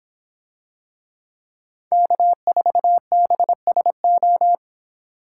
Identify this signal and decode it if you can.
Morse code, heard as a keyed tone, about 25 wpm, 700 Hz. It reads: K4BSO